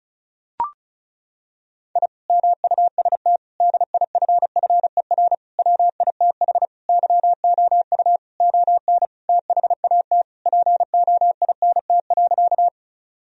Morse code, heard as a keyed tone, about 36 wpm, 700 Hz. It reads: I MUST DIFFER WITH YOU ON THAT POINT.
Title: Sentences from Top 1000 Words - Review Part 1 - 35wpm